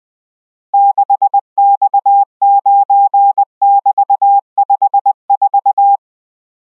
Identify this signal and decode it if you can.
Morse code, heard as a keyed tone, about 20 wpm, 800 Hz. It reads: 6X9<BT>54